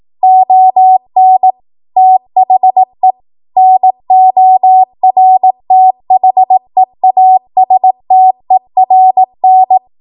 Morse code, heard as a keyed tone, about 18 wpm, 750 Hz. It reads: ON THE NORTHEASTERN
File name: Morse / 18WPM